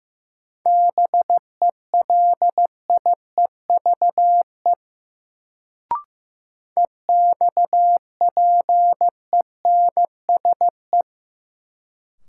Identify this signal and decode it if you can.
Morse code, heard as a keyed tone, about 15 wpm, 700 Hz. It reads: BELIEVE EXPENSE
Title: Sets of 4 Words - New Word Drill - 15wpm